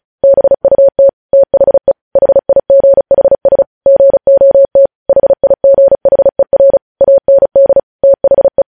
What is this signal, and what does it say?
Morse code, heard as a keyed tone, about 36 wpm, 550 Hz. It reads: BUT THE HIGHS GOT HIGHER AND THE
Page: news headlines in morse code at 35 wpm